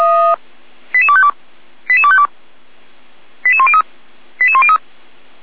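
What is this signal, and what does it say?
Five-tone selective call (ZVEI), heard as quick runs of five tones, tones of about 80 ms. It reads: 88252 88252 88103 88103
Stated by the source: received on SDR, ZVEI five-tone selective call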